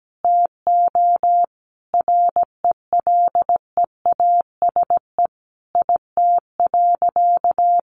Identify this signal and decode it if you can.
Morse code, heard as a keyed tone, about 17 wpm, 700 Hz. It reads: TO RELEASE IT.
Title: Sentences from Top 1100 Words 17wpm